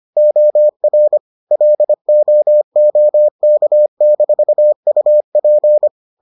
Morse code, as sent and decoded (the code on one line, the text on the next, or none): --- .-. / .-.. --- --- -.- -....- ..- .--.
OR LOOK-UP